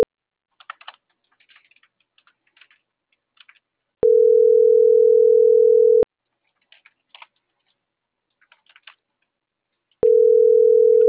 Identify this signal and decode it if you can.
Ringback tone (North America), 440 + 480 Hz.